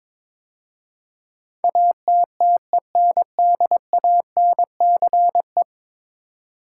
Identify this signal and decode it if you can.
Morse code, heard as a keyed tone, about 22 wpm, 700 Hz. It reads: ATTENDANCE
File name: practice / 22wpm / 10 Letter Words - First 3 Letters Given - 22wpm